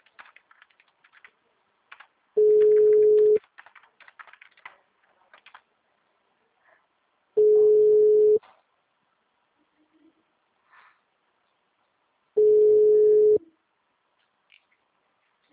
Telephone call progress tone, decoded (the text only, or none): ringback tone